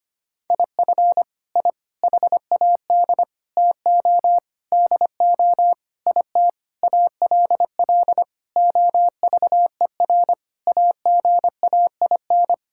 Morse code, heard as a keyed tone, about 25 wpm, 700 Hz. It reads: IF I HAD TO DO IT ALL OVER AGAIN
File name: Sets of 9 Words - 25wpm